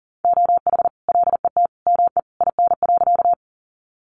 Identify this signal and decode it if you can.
Morse code, heard as a keyed tone, about 40 wpm, 700 Hz. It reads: OH LET ME IN.
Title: Sentences from Top 500 Words 40wpm